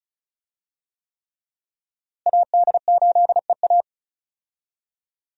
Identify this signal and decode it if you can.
Morse code, heard as a keyed tone, about 36 wpm, 700 Hz. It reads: AD8EA
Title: USA Call Signs - 5 Characters - Repeated 3x - 35wpm